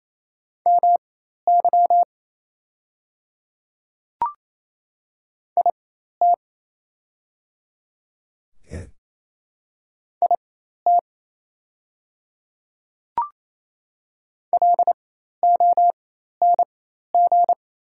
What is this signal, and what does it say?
Morse code, heard as a keyed tone, about 27 wpm, spaced as for 13 wpm, 690 Hz. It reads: MY IT IT LONG
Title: 4x Character Spacing - Top 100 Words - 28wpm